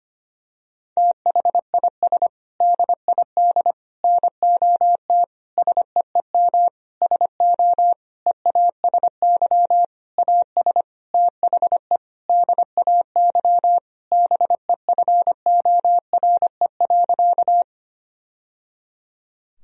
Morse code, heard as a keyed tone, about 25 wpm, 700 Hz. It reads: THIS DID NOT SEEM SO EASY AS THE DAY BEFORE.
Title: Sentences from Top 1000 Words - Review Part 1 - 25wpm